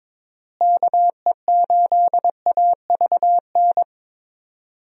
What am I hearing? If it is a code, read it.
Morse code, heard as a keyed tone, about 22 wpm, 700 Hz. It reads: KE8AVN
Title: USA Call Signs - 6 Characters - Repeated 3x - 22wpm